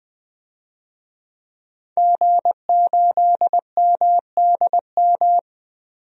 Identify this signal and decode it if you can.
Morse code, heard as a keyed tone, about 20 wpm, 700 Hz. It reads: G8MDM
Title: DXCC-100 Call Signs - 5 Characters - 20wpm